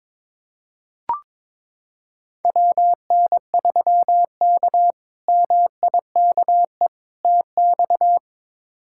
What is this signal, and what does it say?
Morse code, heard as a keyed tone, about 22 wpm, 700 Hz. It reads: WN3K MIKE TX
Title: K1USN SST QSO Practice - 22wpm